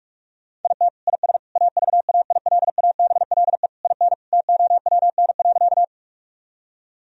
Morse code, heard as a keyed tone, about 46 wpm, 700 Hz. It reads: IT IS AVAILABLE IN TOWN.
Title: Sentences from Top 1000 Words - Review Part 3 - 45wpm